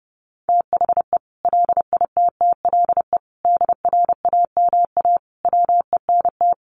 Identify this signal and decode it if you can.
Morse code, heard as a keyed tone, about 30 wpm, 700 Hz. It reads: THE LITTLE DRAMA WENT
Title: Sentences from Top 1500 Words 30wpm